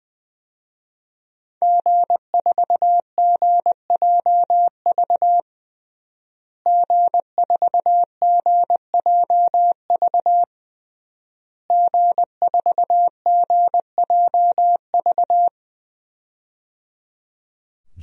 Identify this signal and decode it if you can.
Morse code, heard as a keyed tone, about 20 wpm, 700 Hz. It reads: G4GJV G4GJV G4GJV
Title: DXCC-100 Call Signs - 5 Characters - Repeated 3x - 20wpm